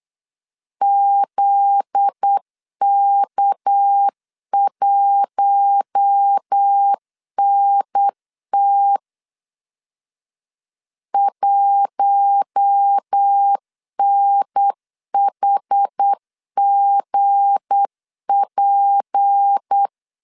Morse code, heard as a keyed tone, about 8 wpm, 800 Hz. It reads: ZK1NT 1NHGP
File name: LEZIONI / 7 GF9Z / 7 GF9Z 8wpm 3213